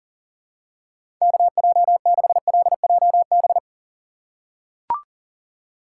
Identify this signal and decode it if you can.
Morse code, heard as a keyed tone, about 40 wpm, 700 Hz. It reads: KJ6LJB
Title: USA Call Signs - 6 Characters - 6x Character Spacing - 40wpm